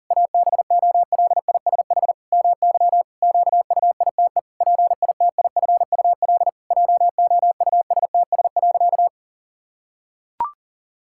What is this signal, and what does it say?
Morse code, heard as a keyed tone, about 37 wpm, 700 Hz. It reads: ABOLISH MY QUITE PITIFUL JOUSTS.
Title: Pangram Challenge v2 - 40wpm